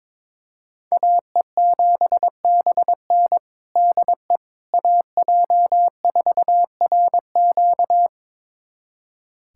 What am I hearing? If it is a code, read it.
Morse code, heard as a keyed tone, about 22 wpm, 700 Hz. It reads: AE7BN DE AJ4RQ
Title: USA Callsign DE Callsign - 5 Characters - 22wpm